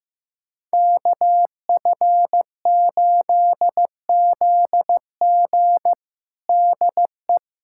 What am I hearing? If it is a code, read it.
Morse code, heard as a keyed tone, about 16 wpm, 700 Hz. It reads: KF8ZG DE
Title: USA Callsign DE Callsign - 5 Characters - 15wpm